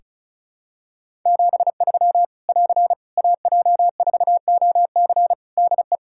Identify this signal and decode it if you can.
Morse code, heard as a keyed tone, about 36 wpm, 700 Hz. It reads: 73 <AR> AJ4OC DE
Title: Practice QSO - Exchange 4 - Ending - 35wpm